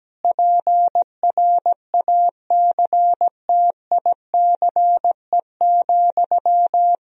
Morse code, heard as a keyed tone, about 17 wpm, 700 Hz. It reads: PRACTICE,
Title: Sentences from Top 1000 Words - Review Part 4 - 17wpm